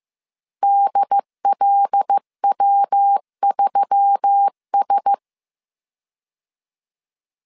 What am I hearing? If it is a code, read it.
Morse code, heard as a keyed tone, about 15 wpm, 800 Hz. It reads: DLW3S
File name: CW / DET / 1 6 1500C S C 14wpm 2951